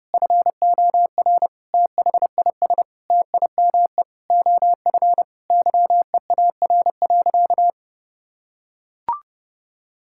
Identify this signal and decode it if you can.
Morse code, heard as a keyed tone, about 30 wpm, 700 Hz. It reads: FOR THIS TIME OF YEAR.
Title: Sentences from Top 1000 Words - Review Part 6 - 30wpm